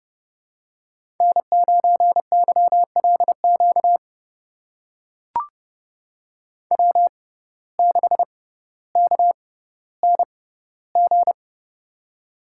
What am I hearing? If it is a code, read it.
Morse code, heard as a keyed tone, about 28 wpm, spaced as for 15 wpm, 700 Hz. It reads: N9YLQ W6KNG
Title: USA Call Signs - 6x Character Spacing - 30wpm